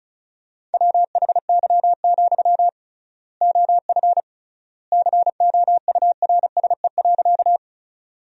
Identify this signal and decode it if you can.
Morse code, heard as a keyed tone, about 36 wpm, 700 Hz. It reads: WHY, OF COURSE.
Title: 3x Word-Spacing - Sentences from Top 500 Words - 35wpm